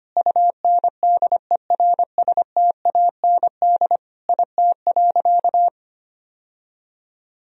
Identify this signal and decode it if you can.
Morse code, heard as a keyed tone, about 25 wpm, 700 Hz. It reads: UNDERSTAND IT.
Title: Sentences from Top 1000 Words - Review Part 3 - 25wpm